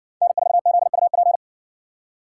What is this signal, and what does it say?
Morse code, heard as a keyed tone, about 60 wpm, 700 Hz. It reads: N4ZUP